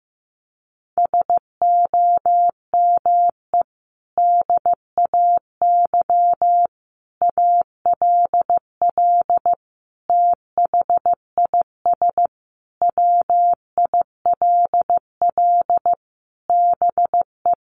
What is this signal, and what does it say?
Morse code, heard as a keyed tone, about 15 wpm, 700 Hz. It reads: SOME DAY ALL THIS WILL BE
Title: Sentences from Top 1300 Words 15wpm